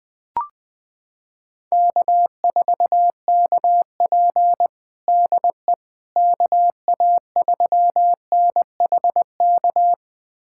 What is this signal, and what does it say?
Morse code, heard as a keyed tone, about 20 wpm, 700 Hz. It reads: K4KP DE KA3NHK
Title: USA Callsign DE Callsign 20wpm